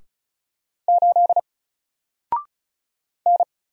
Morse code, heard as a keyed tone, about 33 wpm, 700 Hz. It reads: 8 N
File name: Single-Letter-Number-35wpm